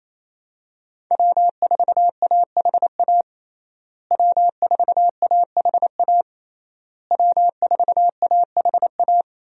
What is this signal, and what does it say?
Morse code, heard as a keyed tone, about 27 wpm, 700 Hz. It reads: W4AHA W4AHA W4AHA
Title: USA Call Signs - 5 Characters - Repeated 3x - 28wpm